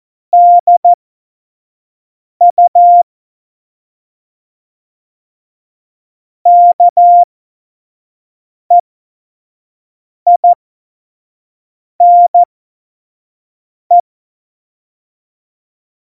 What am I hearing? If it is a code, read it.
Morse code, heard as a keyed tone, about 14 wpm, spaced as for 5 wpm, 700 Hz. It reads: DU KEINE
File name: Morsekurs / Witze / Witz 07. November 2017, 25 F70